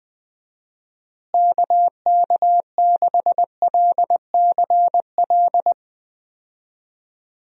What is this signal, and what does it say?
Morse code, heard as a keyed tone, about 20 wpm, 700 Hz. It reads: KK6LCL
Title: USA Call Signs - 6 Characters - Repeated 2x - 20wpm